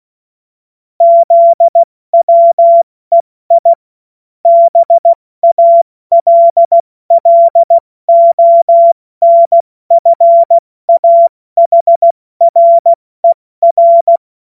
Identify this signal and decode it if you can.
Morse code, heard as a keyed tone, about 16 wpm, 680 Hz. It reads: ZWEI BALLONFAHRER
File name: Morsekurs / Witze / Witz 03. Mai 2022 70 F80